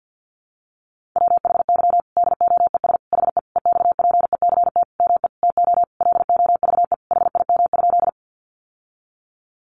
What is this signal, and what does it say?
Morse code, heard as a keyed tone, about 50 wpm, 700 Hz. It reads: WHY DOES HE EXPECT ME TO LOVE HIM?
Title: Sentences from Top 1100 Words 50wpm